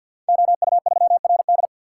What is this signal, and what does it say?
Morse code, heard as a keyed tone, about 51 wpm, 700 Hz. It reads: OU3RD
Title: Code Groups - 5 Letter:Numbers - 50wpm